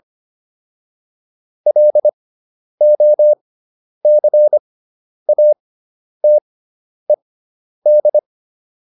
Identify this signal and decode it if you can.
Morse code, heard as a keyed tone, about 25 wpm, spaced as for 10 wpm, 600 Hz. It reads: LOCATED